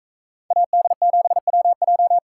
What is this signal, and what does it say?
Morse code, heard as a keyed tone, about 41 wpm, 700 Hz. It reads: AD7WJ